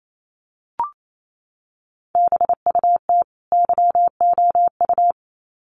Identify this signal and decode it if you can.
Morse code, heard as a keyed tone, about 28 wpm, 700 Hz. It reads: BUT YOU
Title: Sets of 2 Words 28wpm